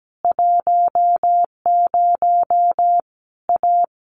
Morse code, heard as a keyed tone, about 17 wpm, 700 Hz. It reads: 10 A
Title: Easy Phrases 17wpm